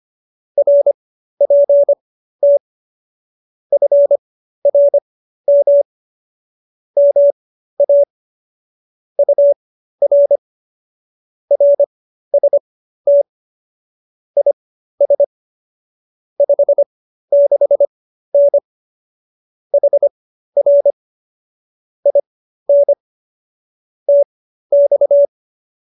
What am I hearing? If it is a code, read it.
Morse code, heard as a keyed tone, about 25 wpm, spaced as for 13 wpm, 570 Hz. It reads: RPT FRM MA UR RST IS 56N HR IN TX